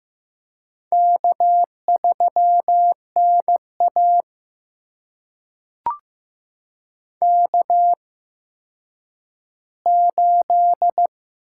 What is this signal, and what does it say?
Morse code, heard as a keyed tone, about 15 wpm, spaced as for 9 wpm, 700 Hz. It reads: K3NA K8
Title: USA Call Signs - 8x Character Spacing - 15wpm